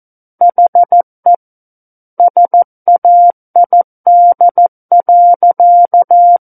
Morse code, heard as a keyed tone, about 14 wpm, 700 Hz. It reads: HE SAID.
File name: Book One / 05-Chapter 5 - 14 WPM